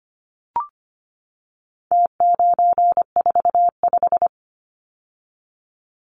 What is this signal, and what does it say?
Morse code, heard as a keyed tone, about 25 wpm, 700 Hz. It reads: T945
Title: Numbers - Contest Style - 4-Digit 25wpm